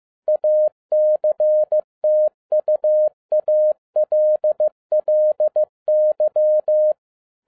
Morse code, heard as a keyed tone, about 15 wpm, 610 Hz. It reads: ACTUALLY
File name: 2011-07-mp3 / n47 JOHN K3TN 15wpm